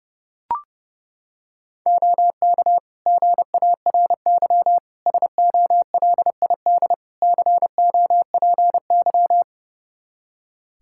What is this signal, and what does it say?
Morse code, heard as a keyed tone, about 32 wpm, 700 Hz. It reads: OK GARY SOLID COPY
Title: Practice QSO - Exchange 3 - More Info - 30wpm